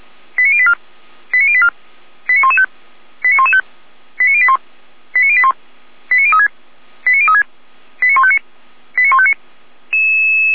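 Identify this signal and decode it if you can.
Five-tone selective call (ZVEI), heard as quick runs of five tones, tones of about 70 ms. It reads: 89084 89084 89115 89115 89081 89081 89036 89036 89159 89159
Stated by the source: ZVEI five-tone selective call 89503, received on SDR.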